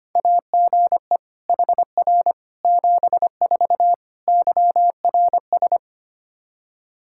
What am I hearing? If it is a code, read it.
Morse code, heard as a keyed tone, about 25 wpm, 700 Hz. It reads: AGE HR 74 YRS
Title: Practice QSO - Exchange 3 - More Info - 25wpm